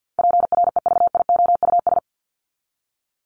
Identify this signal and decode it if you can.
Morse code, heard as a keyed tone, about 50 wpm, 700 Hz. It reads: PREVIOUS
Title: LY Suffix - Top 100 Words with Base Words 50wpm